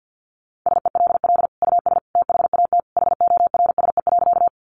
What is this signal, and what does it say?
Morse code, heard as a keyed tone, about 50 wpm, 700 Hz. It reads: SELL US THAT HORSE.
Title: Sentences from Top 1100 Words 50wpm